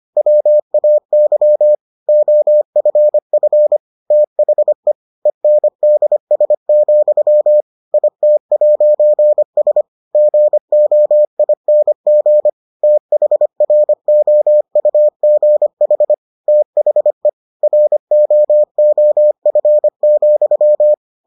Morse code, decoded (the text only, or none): WAY OFF THE ENDS, IT'S GOING THROUGH THE ROOF,